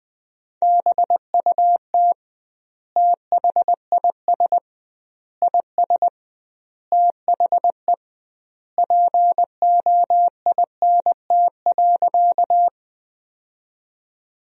Morse code, heard as a keyed tone, about 20 wpm, 700 Hz. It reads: BUT THIS IS THE POINT.
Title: 2x Word-Spacing - Sentences from Top 200 Words - 20wpm